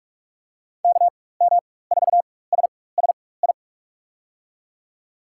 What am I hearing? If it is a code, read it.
Morse code, heard as a keyed tone, about 41 wpm, spaced as for 20 wpm, 700 Hz. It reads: KM4SSI
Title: USA Call Signs - 6 Characters - 4x Character Spacing - 45wpm